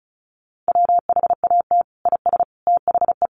Morse code, heard as a keyed tone, about 34 wpm, 700 Hz. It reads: WHAT IS THE
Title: Sentences from Top 1500 Words 35wpm